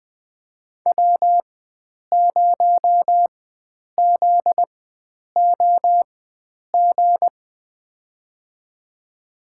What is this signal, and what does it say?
Morse code, heard as a keyed tone, about 20 wpm, spaced as for 9 wpm, 700 Hz. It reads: W0ZOG
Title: USA Call Signs - 5 Characters - 4x Character Spacing - 20wpm